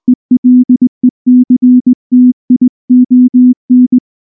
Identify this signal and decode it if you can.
Morse code, heard as a keyed tone, about 18 wpm, 250 Hz. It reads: ELECTION
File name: morse / morse-training-20WPM-1